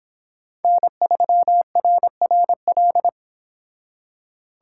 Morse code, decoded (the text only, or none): N3RRL